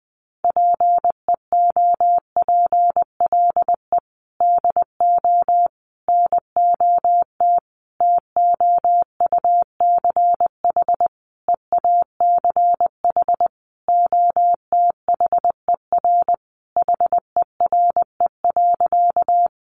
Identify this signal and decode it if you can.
Morse code, heard as a keyed tone, about 20 wpm, 700 Hz. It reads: PEOPLE DO NOT TOUCH EACH OTHER HERE.
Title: Sentences from Top 1200 Words 20wpm